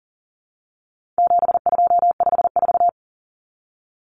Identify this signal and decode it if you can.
Morse code, heard as a keyed tone, about 39 wpm, 700 Hz. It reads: 7254
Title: Common Cut Numbers - 4-Digit 40wpm